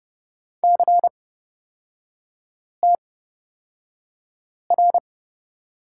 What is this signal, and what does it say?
Morse code, heard as a keyed tone, about 30 wpm, spaced as for 5 wpm, 700 Hz. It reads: CTR
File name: Lesson 38e - Letter q - Three Characters 30wpm